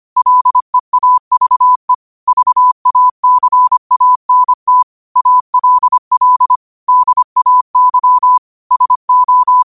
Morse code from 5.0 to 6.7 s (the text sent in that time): ALL